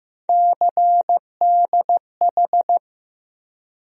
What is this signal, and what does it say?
Morse code, heard as a keyed tone, about 15 wpm, 700 Hz. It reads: CDH
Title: USA Call Sign Postfixes - 3 Letters 15wpm